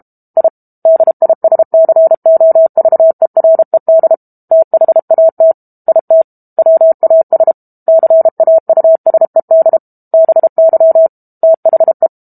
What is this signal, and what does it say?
Morse code, heard as a keyed tone, about 32 wpm, 650 Hz. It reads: I DISCOVERED THAT IT WAS CAUSED BY THE
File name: Book Two / 06-Chapter 6 - 32 WPM